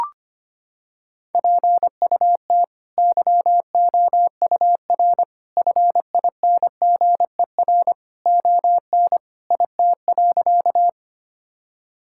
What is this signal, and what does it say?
Morse code, heard as a keyed tone, about 25 wpm, 700 Hz. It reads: PUT YOUR FINGER ON IT.
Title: Sentences from Top 1600 Words 25wpm